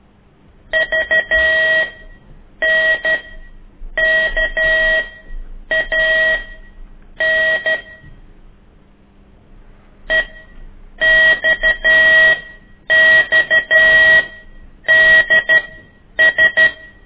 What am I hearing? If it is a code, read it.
Morse code, heard as a keyed tone, about 12 wpm, spaced as for 8 wpm, 1900 Hz. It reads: VNKAN EXXDS
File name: ik1weg2 / CORSI CW